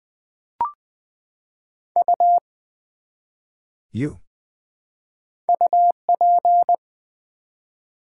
Morse code, heard as a keyed tone, about 20 wpm, 700 Hz. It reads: U UP
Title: Build A Word - 1 Letter At A Time - 5-Letter Words - 20wpm